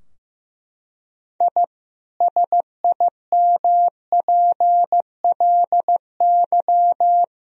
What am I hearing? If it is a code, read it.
Morse code, heard as a keyed tone, about 15 wpm, 700 Hz. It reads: I SIMPLY
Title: Sentences from Top 1000 Words - Review Part 4 - 15wpm